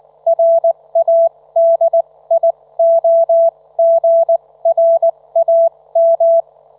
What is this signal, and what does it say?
Morse code, heard as a keyed tone, about 19 wpm, spaced as for 16 wpm, 670 Hz. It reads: RADIOGRAM
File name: Trivia